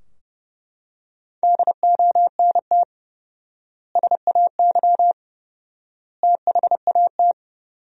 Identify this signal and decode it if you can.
Morse code, heard as a keyed tone, about 31 wpm, 700 Hz. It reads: DONT SAY THAT
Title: 4x Word-Spacing - Sentences from Top 300 Words - 30wpm